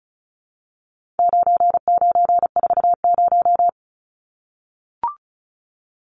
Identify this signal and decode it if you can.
Morse code, heard as a keyed tone, about 34 wpm, 700 Hz. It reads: 9940
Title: Numbers - 4-Digit 35wpm